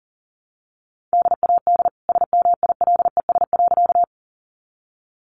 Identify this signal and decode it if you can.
Morse code, heard as a keyed tone, about 40 wpm, 700 Hz. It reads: DAD SMILES.
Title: Easy Phrases 40wpm